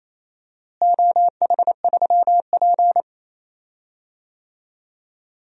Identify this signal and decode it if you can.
Morse code, heard as a keyed tone, about 28 wpm, 700 Hz. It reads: OH3P
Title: DXCC-100 Call Signs - 4 Characters - Repeated 2x - 28wpm